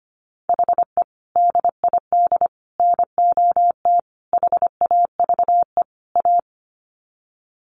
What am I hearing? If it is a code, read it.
Morse code, heard as a keyed tone, about 25 wpm, 700 Hz. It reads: HE DID NOT HAVE A
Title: Sets of 5 Words Encore 25wpm